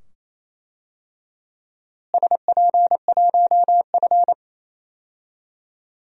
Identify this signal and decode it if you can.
Morse code, heard as a keyed tone, about 28 wpm, 700 Hz. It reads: SP1F
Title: DXCC-100 Call Signs - 4 Characters - Repeated 2x - 28wpm